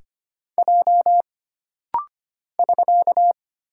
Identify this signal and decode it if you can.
Morse code, heard as a keyed tone, about 25 wpm, 700 Hz. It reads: J <SK>
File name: Single Character (Letter:Number:Punctuation:Prosign) - Rapid-Fire - 25wpm